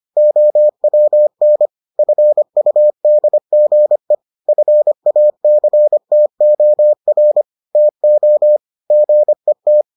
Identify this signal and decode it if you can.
Morse code, heard as a keyed tone, about 25 wpm, 600 Hz. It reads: OWN FUDGE FACTOR TO GET